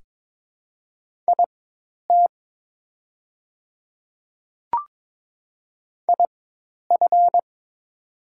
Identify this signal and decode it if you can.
Morse code, heard as a keyed tone, about 22 wpm, spaced as for 10 wpm, 700 Hz. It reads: IT IF
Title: 4x Character Spacing - Top 100 Words - 22wpm